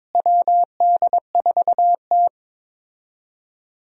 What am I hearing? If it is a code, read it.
Morse code, heard as a keyed tone, about 22 wpm, 700 Hz. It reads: WD4T
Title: USA Call Signs - 4 Characters - Repeated 3x - 22wpm